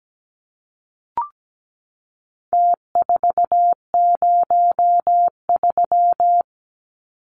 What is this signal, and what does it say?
Morse code, heard as a keyed tone, about 17 wpm, 700 Hz. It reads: T403
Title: Numbers - Contest Style - 4-Digit 17wpm